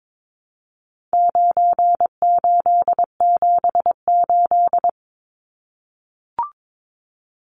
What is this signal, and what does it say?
Morse code, heard as a keyed tone, about 22 wpm, 700 Hz. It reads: 9878
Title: Numbers - 4-Digit 22wpm